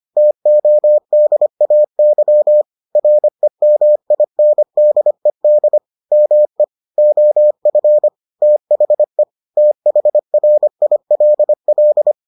Morse code, heard as a keyed tone, about 25 wpm, 600 Hz. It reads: TODAY REMINDED ME OF THE THRILL